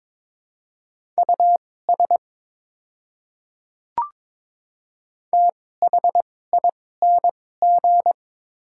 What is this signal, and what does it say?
Morse code, heard as a keyed tone, about 22 wpm, spaced as for 16 wpm, 700 Hz. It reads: US THING
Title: 2x Character Spacing - Top 200 Words - 22wpm